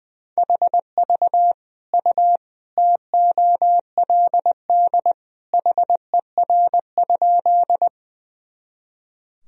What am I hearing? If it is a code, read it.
Morse code, heard as a keyed tone, about 20 wpm, 700 Hz. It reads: HV U TOLD HER?
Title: Sentences from Top 400 Words and Abbreviations 20wpm